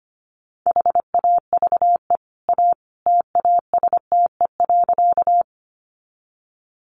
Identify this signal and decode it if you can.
Morse code, heard as a keyed tone, about 25 wpm, 700 Hz. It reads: HAVE A TASTE.
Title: Sentences from Top 1100 Words 25wpm